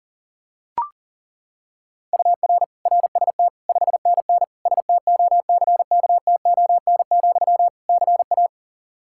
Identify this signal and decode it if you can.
Morse code, heard as a keyed tone, about 40 wpm, 700 Hz. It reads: UR RST 5NN STOCKTON, CA
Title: K1USN SST QSO Practice 40wpm